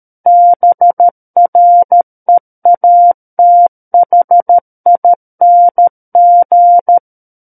Morse code, heard as a keyed tone, about 13 wpm, 700 Hz. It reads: BREATHING
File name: Book One / 04-Chapter 4 - 13 WPM